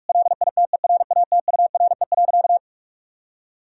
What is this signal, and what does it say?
Morse code, heard as a keyed tone, about 46 wpm, 700 Hz. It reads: LITERATURE.
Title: Sentences from Top 1000 Words - Review Part 3 - 45wpm